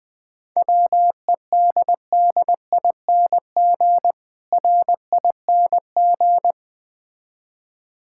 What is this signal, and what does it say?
Morse code, heard as a keyed tone, about 20 wpm, 700 Hz. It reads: WEDDING RING
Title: Difficult Words - Speed-Racing with Hints - 20wpm